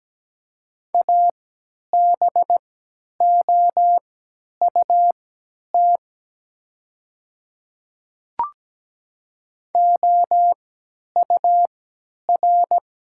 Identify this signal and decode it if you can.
Morse code, heard as a keyed tone, about 17 wpm, spaced as for 10 wpm, 700 Hz. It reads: ABOUT OUR
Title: 3x Character Spacing - Top 300 Words - 17wpm